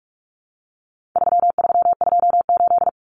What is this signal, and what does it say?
Morse code, heard as a keyed tone, about 44 wpm, 700 Hz. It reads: 3328
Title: Common Cut Numbers - 4-Digit 45wpm